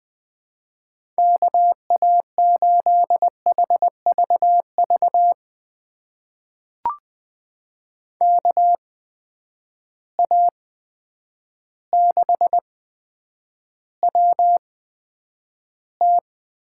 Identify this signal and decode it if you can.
Morse code, heard as a keyed tone, about 20 wpm, spaced as for 9 wpm, 700 Hz. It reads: KA8HVV KA6WT
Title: USA Call Signs - 6 Characters - 8x Character Spacing - 20wpm